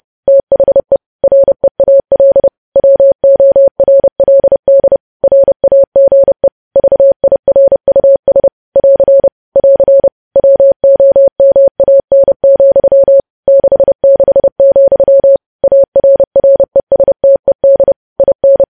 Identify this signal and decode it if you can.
Morse code, heard as a keyed tone, about 30 wpm, 550 Hz. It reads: THE REAL WORLD RAGE VIRUS <AR> <AR> WOMAN, 66, ARRESTED IN
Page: news headlines in morse code at 30 wpm